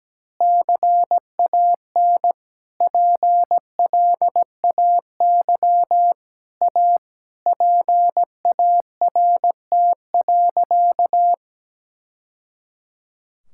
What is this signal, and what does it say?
Morse code, heard as a keyed tone, about 17 wpm, 700 Hz. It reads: CAN PLAY A PART.
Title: Sentences from Top 1000 Words - Review Part 5 - 17wpm